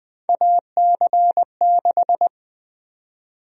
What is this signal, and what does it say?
Morse code, heard as a keyed tone, about 20 wpm, 700 Hz. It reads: AC6
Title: USA Call Sign Prefixes - Repeated 2x - 20wpm